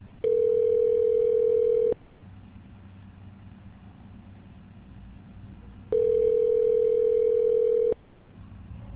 Ringback tone (North America), 440 + 480 Hz.